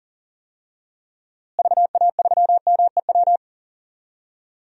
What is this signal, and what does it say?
Morse code, heard as a keyed tone, about 40 wpm, 700 Hz. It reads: VA3MEW
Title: DXCC-100 Call Signs - 6 Characters - Repeated 3x - 40wpm